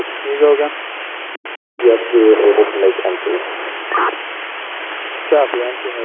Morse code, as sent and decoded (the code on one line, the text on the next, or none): none